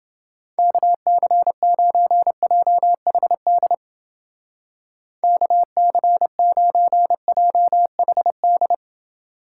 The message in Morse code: KC9JHD KC9JHD

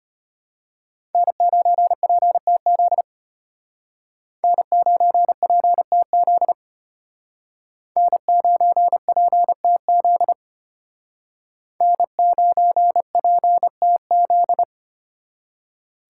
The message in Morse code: N9PTZ N9PTZ N9PTZ N9PTZ

